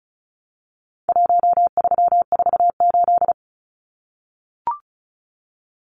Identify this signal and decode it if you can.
Morse code, heard as a keyed tone, about 34 wpm, 700 Hz. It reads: 1348
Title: Numbers - Contest Style - 4-Digit 35wpm